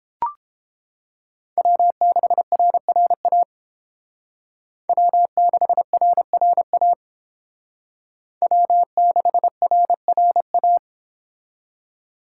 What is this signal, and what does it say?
Morse code, heard as a keyed tone, about 30 wpm, 700 Hz. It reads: W6RRA W6RRA W6RRA